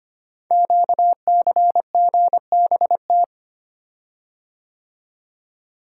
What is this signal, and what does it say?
Morse code, heard as a keyed tone, about 25 wpm, 700 Hz. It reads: QCGBT
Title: Code Groups - 5 Letter:Numbers - 25wpm